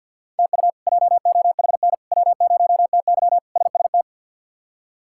Morse code, heard as a keyed tone, about 48 wpm, 700 Hz. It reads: TU JOHN W0TY SST
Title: K1USN SST QSO Practice - 50wpm